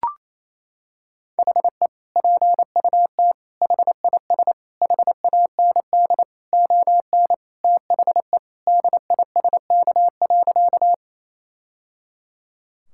Morse code, heard as a keyed tone, about 28 wpm, 700 Hz. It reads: HE PUT HIS HAND ON THE DISK.
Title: Sentences from Top 1000 Words - Review Part 6 - 28wpm